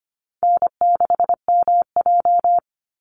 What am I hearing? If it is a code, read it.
Morse code, heard as a keyed tone, about 25 wpm, 700 Hz. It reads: N6MJ
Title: Easy Call Signs - 25wpm